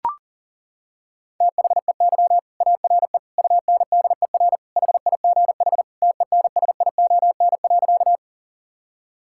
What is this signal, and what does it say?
Morse code, heard as a keyed tone, about 40 wpm, 700 Hz. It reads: THEY ARE UNDER HIGH TENSION.
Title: Sentences from Top 1600 Words 40wpm